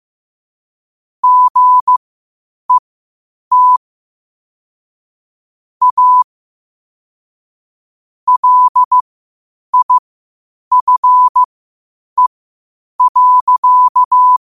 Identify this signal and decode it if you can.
Morse code, heard as a keyed tone, about 15 wpm, spaced as for 8 wpm, 1000 Hz. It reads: GET A LIFE.
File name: qotdfiles / 8wpm